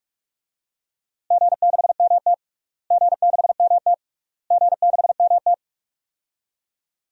Morse code, heard as a keyed tone, about 44 wpm, 700 Hz. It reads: G6MT G6MT G6MT
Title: DXCC-100 Call Signs - 4 Characters - Repeated 3x - 45wpm